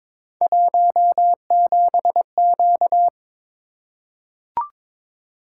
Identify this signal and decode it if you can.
Morse code, heard as a keyed tone, about 22 wpm, 700 Hz. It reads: 17Q
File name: practice / 22wpm / Three Letter-Number Combinations - 22wpm